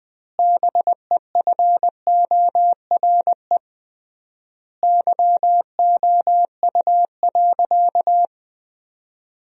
Morse code, heard as a keyed tone, about 20 wpm, 700 Hz. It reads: BEFORE YOU.
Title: 3x Word-Spacing - Sentences from Top 200 Words - 20wpm